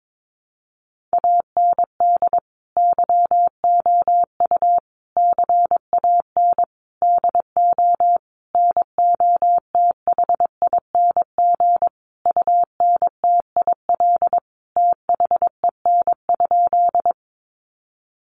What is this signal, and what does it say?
Morse code, heard as a keyed tone, about 22 wpm, 700 Hz. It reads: AND YOU CAN DO NOTHING UNTIL THEN?
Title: Sentences from Top 500 Words 22wpm